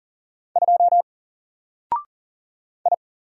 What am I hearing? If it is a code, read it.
Morse code, heard as a keyed tone, about 40 wpm, 700 Hz. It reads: 2 I
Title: Single Character (Letter:Number:Punctuation:Prosign) Alt 40wpm